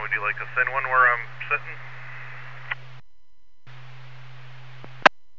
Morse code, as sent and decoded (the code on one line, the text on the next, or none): none